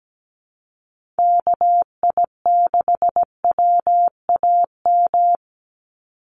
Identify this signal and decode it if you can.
Morse code, heard as a keyed tone, about 17 wpm, 700 Hz. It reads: KI6WAM
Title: USA Call Signs 17wpm